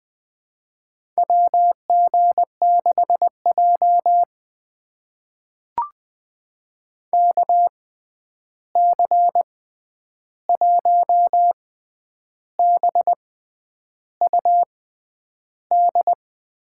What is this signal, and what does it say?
Morse code, heard as a keyed tone, about 21 wpm, spaced as for 9 wpm, 700 Hz. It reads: WG6J KC1BUD
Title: USA Call Signs - 6x Character Spacing - 20wpm